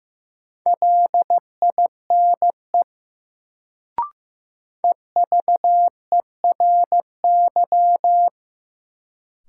Morse code, heard as a keyed tone, about 15 wpm, 700 Hz. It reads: LINE EVERY
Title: Sets of 5 Words - New Word Drill - 15wpm